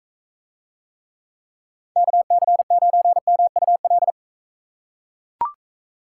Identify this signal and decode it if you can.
Morse code, heard as a keyed tone, about 40 wpm, 700 Hz. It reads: KC9MUL